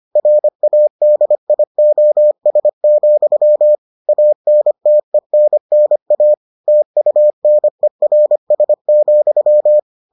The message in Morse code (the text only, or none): RADIOS, ANTENNA TUNERS,